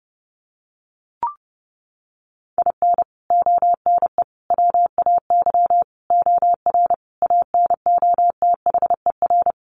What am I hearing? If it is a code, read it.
Morse code, heard as a keyed tone, about 30 wpm, 700 Hz. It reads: IN ONE WAY OR ANOTHER
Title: Sets of 5 Words Encore 30wpm